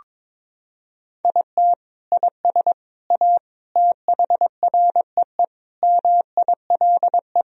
Morse code, heard as a keyed tone, about 22 wpm, 700 Hz. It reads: IT IS A THREE MILE
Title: Sentences from Top 1000 Words - Review Part 1 - 22wpm